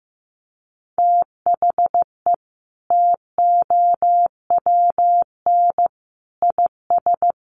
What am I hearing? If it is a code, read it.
Morse code, heard as a keyed tone, about 15 wpm, 700 Hz. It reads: THE TOWN IS
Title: Sentences from Top 1100 Words 15wpm